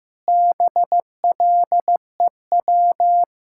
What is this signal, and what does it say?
Morse code, heard as a keyed tone, about 15 wpm, 700 Hz. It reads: BLEW
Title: Pangram Challenge v2 - 15wpm